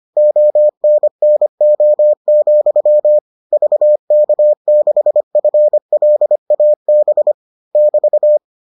Morse code, heard as a keyed tone, about 25 wpm, 600 Hz. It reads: ONNO, VK6FLAB <BT>